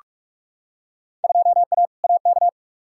Morse code, heard as a keyed tone, about 44 wpm, 700 Hz. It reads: 2A AK
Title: Field Day Exchanges Encore - 45wpm